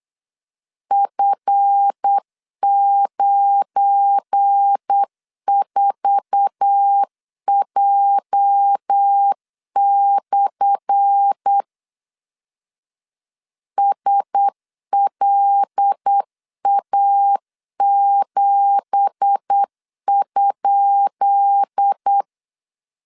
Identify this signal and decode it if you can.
Morse code, heard as a keyed tone, about 8 wpm, 800 Hz. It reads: F94J/ SLA7?
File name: LEZIONI / 10 SIGNS / 10D COMA 8wpm 3749